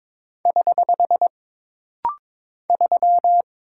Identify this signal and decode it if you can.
Morse code, heard as a keyed tone, about 22 wpm, 700 Hz. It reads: <HH> 3
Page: This rapid-fire timing is geared to push your ICR (Instant Character Recognition) skills.